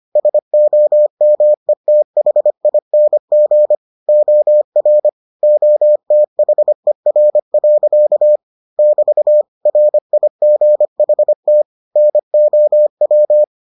Morse code, heard as a keyed tone, about 25 wpm, 600 Hz. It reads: SOMETHING OR OTHER. <BT> RIGHT NOW